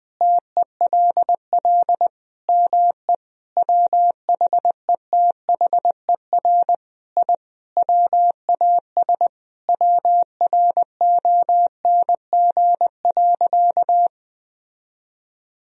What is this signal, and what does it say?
Morse code, heard as a keyed tone, about 21 wpm, 700 Hz. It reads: TELL ME WHETHER I WAS WRONG.
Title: Sentences from Top 1000 Words - Review Part 3 - 20wpm